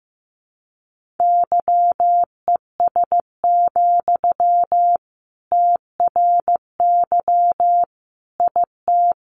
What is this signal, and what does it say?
Morse code, heard as a keyed tone, about 15 wpm, 700 Hz. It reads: YES, TRY IT
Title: Sentences from Top 600 Words 15wpm